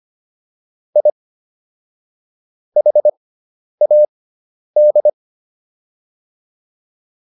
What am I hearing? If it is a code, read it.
Morse code, heard as a keyed tone, about 25 wpm, spaced as for 10 wpm, 600 Hz. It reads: I HAD